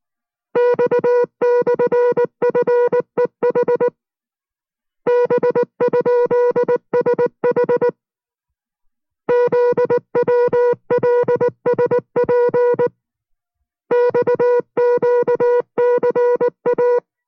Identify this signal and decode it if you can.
Morse code, heard as a keyed tone, about 19 wpm, 490 Hz. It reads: X/FEH B?SH ZWLSP XQCA